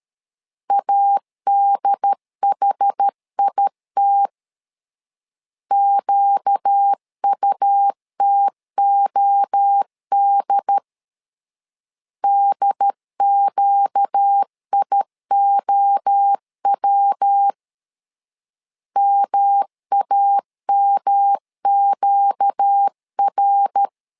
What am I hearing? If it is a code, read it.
Morse code, heard as a keyed tone, about 13 wpm, 800 Hz. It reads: ADHIT QUTOD DQIOW MAMQR
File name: LEZIONI / 4 Q 12wpm 1833